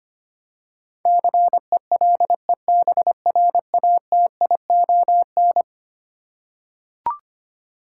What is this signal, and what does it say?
Morse code, heard as a keyed tone, about 25 wpm, 700 Hz. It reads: CELEBRATION
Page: morse-code-ninja-practice-25wpm